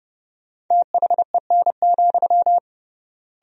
Morse code, heard as a keyed tone, about 30 wpm, 700 Hz. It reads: THEN,
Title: 3x Word-Spacing - Sentences from Top 100 Words - 30wpm